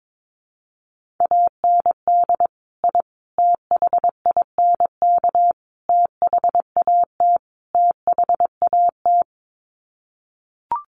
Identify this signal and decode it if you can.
Morse code, heard as a keyed tone, about 22 wpm, 700 Hz. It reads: AND I THINK THAT THAT
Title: Sets of 5 Words Encore 22wpm